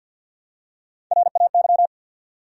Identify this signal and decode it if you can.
Morse code, heard as a keyed tone, about 52 wpm, 700 Hz. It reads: RAY
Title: USA Common Names - Max 4 Letters - 50wpm